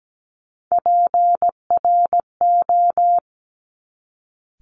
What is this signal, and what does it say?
Morse code, heard as a keyed tone, about 17 wpm, 700 Hz. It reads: PRO
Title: Top Three Letter Combinations in Words 17wpm